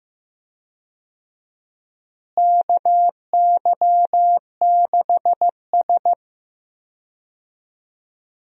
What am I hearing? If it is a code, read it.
Morse code, heard as a keyed tone, about 15 wpm, 700 Hz. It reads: KY6S
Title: USA Call Signs - 4 Characters - Repeated 3x - 15wpm